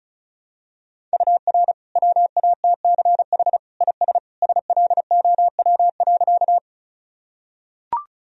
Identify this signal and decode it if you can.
Morse code, heard as a keyed tone, about 36 wpm, 700 Hz. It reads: UR WATCH IS SLOW.
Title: Sentences from Top 400 Words and Abbreviations 35wpm